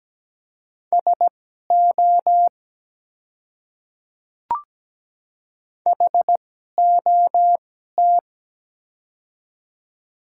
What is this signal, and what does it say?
Morse code, heard as a keyed tone, about 17 wpm, spaced as for 12 wpm, 700 Hz. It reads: SO HOT
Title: 2x Character Spacing - Top 100 Words - 17wpm